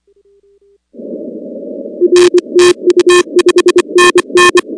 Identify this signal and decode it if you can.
Morse code, heard as a keyed tone, about 24 wpm, 360 Hz. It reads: R TU 5NN
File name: contests / 2019-cw-wpx-kt5j-k5pi- -2330 0-19